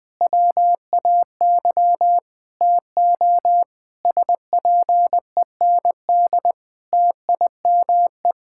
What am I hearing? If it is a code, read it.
Morse code, heard as a keyed tone, about 20 wpm, 700 Hz. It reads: WAY TO SPEND TIME
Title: Difficult Sentences with Hints - 20wpm